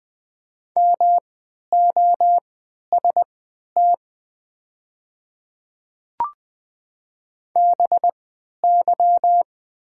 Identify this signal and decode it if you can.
Morse code, heard as a keyed tone, about 20 wpm, spaced as for 11 wpm, 700 Hz. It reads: MOST BY
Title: 3x Character Spacing - Top 200 Words - 20wpm